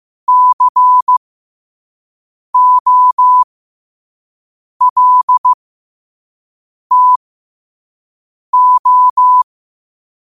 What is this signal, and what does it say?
Morse code, heard as a keyed tone, about 15 wpm, spaced as for 5 wpm, 1000 Hz. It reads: COLTO
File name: qotdfiles / 5wpm